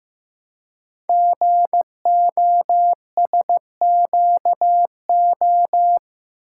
Morse code, heard as a keyed tone, about 15 wpm, 700 Hz. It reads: GOSQO
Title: Code Groups - 5 Letters - 15wpm